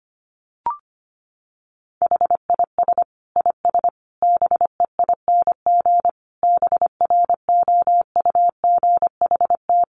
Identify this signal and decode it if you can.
Morse code, heard as a keyed tone, about 25 wpm, 700 Hz. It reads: HIS IS BEING BROUGHT
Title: Sentences from Top 600 Words 25wpm